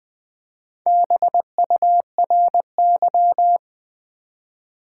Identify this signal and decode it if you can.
Morse code, heard as a keyed tone, about 20 wpm, 700 Hz. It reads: BURY
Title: Build A Word - 1 Letter At A Time - 3 to 5-Letter Words - 20wpm